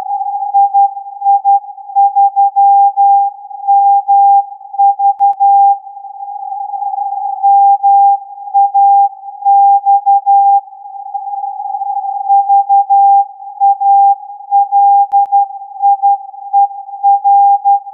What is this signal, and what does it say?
Morse code, heard as a keyed tone, about 12 wpm, 800 Hz. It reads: II3MV MAX VALIER
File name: MAX-VALIER-SAT-TURN-AIRSPY-02-R